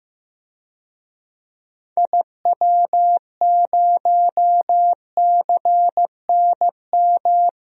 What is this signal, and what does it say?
Morse code, heard as a keyed tone, about 15 wpm, 700 Hz. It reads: IW0CNM
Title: DXCC-100 Call Signs - 6 Characters - 15wpm